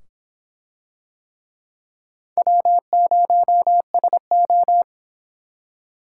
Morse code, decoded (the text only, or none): W0SO